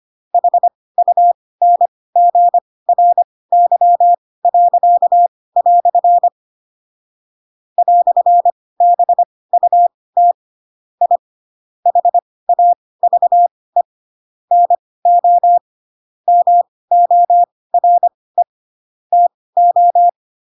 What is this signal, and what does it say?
Morse code, heard as a keyed tone, about 25 wpm, spaced as for 18 wpm, 700 Hz. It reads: HUNGRY." "BUT I HAVE NO MORE TO
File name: Pinocchio / Pinocchio ch7 12 25 ik1hge